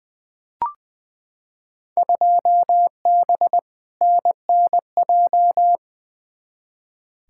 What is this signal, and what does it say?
Morse code, heard as a keyed tone, about 21 wpm, 700 Hz. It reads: 2B NNJ
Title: Field Day Exchanges Encore - 20wpm